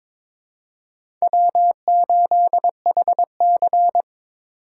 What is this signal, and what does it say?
Morse code, heard as a keyed tone, about 23 wpm, 700 Hz. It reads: W8HC
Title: USA Call Signs - 4 Characters - 4x Character Spacing - 22wpm